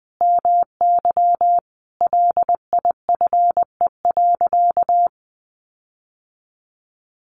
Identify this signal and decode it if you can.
Morse code, heard as a keyed tone, about 20 wpm, 700 Hz. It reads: MY LIFE.
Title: Sentences from Top 1500 Words 20wpm